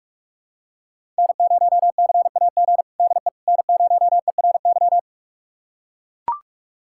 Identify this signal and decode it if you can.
Morse code, heard as a keyed tone, about 41 wpm, 700 Hz. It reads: N0CAG DE N0ERY
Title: USA Callsign DE Callsign - 5 Characters - 45wpm